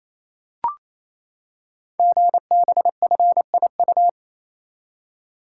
Morse code, heard as a keyed tone, about 29 wpm, 700 Hz. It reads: GBFIU
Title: Code Groups - 5 Letters - 28wpm